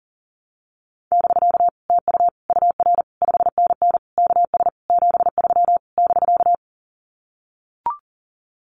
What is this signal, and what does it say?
Morse code, heard as a keyed tone, about 40 wpm, 700 Hz. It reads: <BK> TU UR 5NN KS 73 <BK>
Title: Common QSO Phrases Encore 40wpm